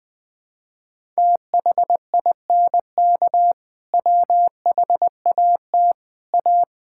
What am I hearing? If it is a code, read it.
Morse code, heard as a keyed tone, about 21 wpm, 700 Hz. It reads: THINK WHAT A
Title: Sentences from Top 1000 Words - Review Part 5 - 20wpm